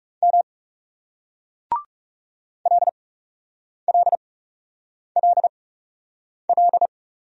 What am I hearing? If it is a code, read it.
Morse code, heard as a keyed tone, about 40 wpm, spaced as for 8 wpm, 700 Hz. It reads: M LLLL